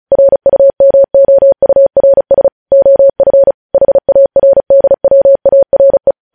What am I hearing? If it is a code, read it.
Morse code, heard as a keyed tone, about 36 wpm, 550 Hz. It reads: RUMOURS OF HARDWARE
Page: news headlines in morse code at 35 wpm